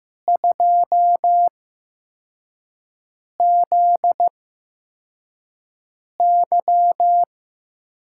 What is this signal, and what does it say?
Morse code, heard as a keyed tone, about 15 wpm, spaced as for 4 wpm, 700 Hz. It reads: 2ZY
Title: USA Call Signs - 5 Characters - 8x Character Spacing - 15wpm